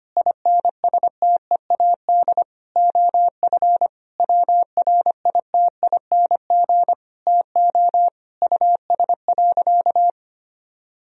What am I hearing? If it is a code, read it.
Morse code, heard as a keyed tone, about 25 wpm, 700 Hz. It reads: INSTEAD OF WRITING TO US.
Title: Sentences from Top 1000 Words - Review Part 4 - 25wpm